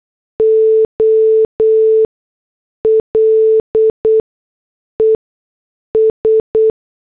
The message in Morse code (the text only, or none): OLES